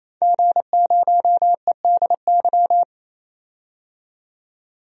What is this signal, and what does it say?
Morse code, heard as a keyed tone, about 27 wpm, 700 Hz. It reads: G0EDY
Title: DXCC-100 Call Signs - 5 Characters - 28wpm